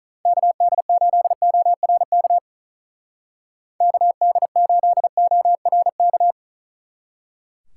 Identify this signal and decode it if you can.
Morse code, heard as a keyed tone, about 43 wpm, 700 Hz. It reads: KD8ORK KD8ORK
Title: Speed Racing - USA Call Signs 35wpm